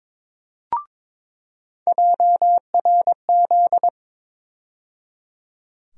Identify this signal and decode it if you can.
Morse code, heard as a keyed tone, about 22 wpm, 700 Hz. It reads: JRZ
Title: USA Call Sign Postfixes - 3 Letters 22wpm